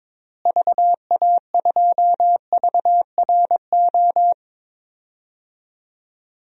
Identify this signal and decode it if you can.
Morse code, heard as a keyed tone, about 22 wpm, 700 Hz. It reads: VA2VRO
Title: DXCC-100 Call Signs - 6 Characters - Repeated 2x - 22wpm